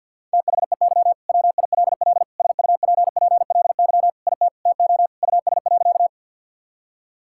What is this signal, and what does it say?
Morse code, heard as a keyed tone, about 50 wpm, 700 Hz. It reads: THEY WILL SUPPLY IT TO US.
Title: Sentences from Top 1000 Words - Review Part 5 - 50wpm